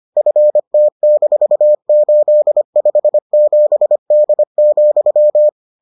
Morse code, heard as a keyed tone, about 25 wpm, 600 Hz. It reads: FT-857D,